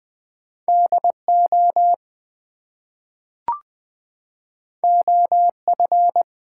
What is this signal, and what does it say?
Morse code, heard as a keyed tone, about 20 wpm, 700 Hz. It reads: DO OF
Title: Two Letter Words - 20wpm